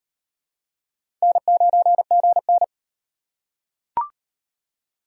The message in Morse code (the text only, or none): N9GN